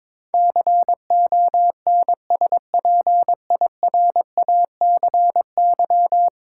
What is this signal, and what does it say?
Morse code, heard as a keyed tone, about 22 wpm, 700 Hz. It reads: CONSPIRACY
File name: practice / 22wpm / 10 Letter Words - First 3 Letters Given - 22wpm